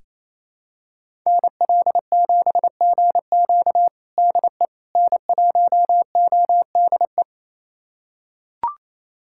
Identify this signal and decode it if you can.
Morse code, heard as a keyed tone, about 28 wpm, 700 Hz. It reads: NL7GQ DE N1ODE
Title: USA Callsign DE Callsign - 5 Characters - 28wpm